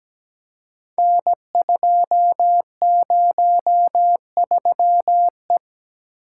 Morse code, heard as a keyed tone, about 17 wpm, 700 Hz. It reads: N203E
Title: Code Groups - 5 Letter:Numbers - 17wpm